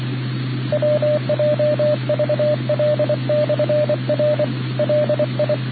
Morse code, heard as a keyed tone, about 24 wpm, 600 Hz, with radio noise. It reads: WJVL/R LI